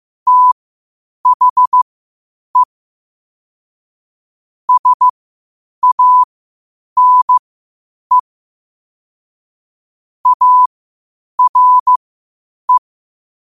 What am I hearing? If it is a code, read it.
Morse code, heard as a keyed tone, about 15 wpm, spaced as for 8 wpm, 1000 Hz. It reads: THE SANE ARE
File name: qotdfiles / 8wpm